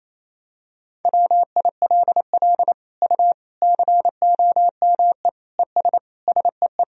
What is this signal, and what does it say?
Morse code, heard as a keyed tone, about 28 wpm, 700 Hz. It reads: WILL U COME ES SEE
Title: Sentences from Top 100 Words and Abbreviations 28wpm